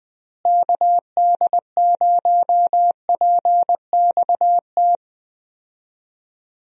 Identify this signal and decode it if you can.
Morse code, heard as a keyed tone, about 21 wpm, 700 Hz. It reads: KD0PXT
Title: USA Call Signs - 6 Characters - Repeated 2x - 20wpm